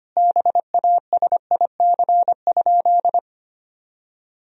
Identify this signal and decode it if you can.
Morse code, heard as a keyed tone, about 25 wpm, 700 Hz. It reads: BASIC?
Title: Sentences from Top 1000 Words - Review Part 3 - 25wpm